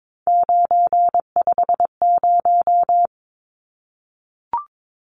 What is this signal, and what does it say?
Morse code, heard as a keyed tone, about 22 wpm, 700 Hz. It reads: 950
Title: Numbers - 3-Digit 22wpm